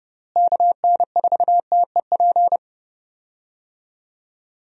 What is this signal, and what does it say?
Morse code, heard as a keyed tone, about 30 wpm, 700 Hz. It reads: KN4TEP
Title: USA Call Signs - 6 Characters - Repeated 3x - 30wpm